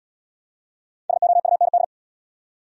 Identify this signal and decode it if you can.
Morse code, heard as a keyed tone, about 64 wpm, 700 Hz. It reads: SOUND